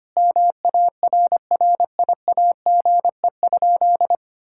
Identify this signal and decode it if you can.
Morse code, heard as a keyed tone, about 25 wpm, 700 Hz. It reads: MARRIAGE?
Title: Sentences from Top 1000 Words - Review Part 5 - 25wpm